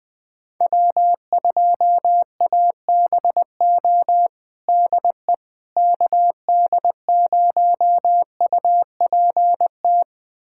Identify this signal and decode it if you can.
Morse code, heard as a keyed tone, about 20 wpm, 700 Hz. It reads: W2ABO DE KD0UPT
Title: USA Callsign DE Callsign 20wpm